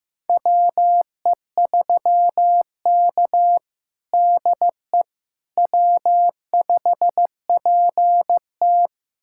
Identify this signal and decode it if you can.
Morse code, heard as a keyed tone, about 15 wpm, 700 Hz. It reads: WE3K DE W5PT
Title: USA Callsign DE Callsign - 4 Characters - 15wpm